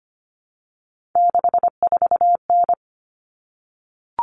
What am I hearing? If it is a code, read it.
Morse code, heard as a keyed tone, about 25 wpm, 700 Hz. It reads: 64N
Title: Common Cut Numbers - 3-Digit 25wpm